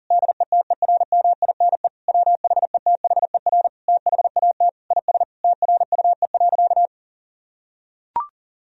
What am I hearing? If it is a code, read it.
Morse code, heard as a keyed tone, about 40 wpm, 700 Hz. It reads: DETERMINE WHETHER THAT IS TRUE.
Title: Sentences from Top 1000 Words - Review Part 4 - 40wpm